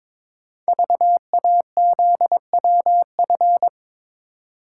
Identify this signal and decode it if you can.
Morse code, heard as a keyed tone, about 22 wpm, 700 Hz. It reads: VAZWF
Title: Code Groups - 5 Letter:Numbers - 22wpm